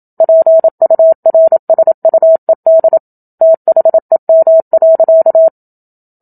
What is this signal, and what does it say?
Morse code, heard as a keyed tone, about 27 wpm, 650 Hz. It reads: PURSUED THEM.
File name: Book Two / 01-Chapter 1 - 27 WPM